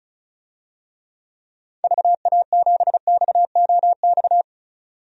Morse code, heard as a keyed tone, about 36 wpm, 690 Hz. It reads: VA7XOX
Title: DXCC-100 Call Signs - 6 Characters - 35wpm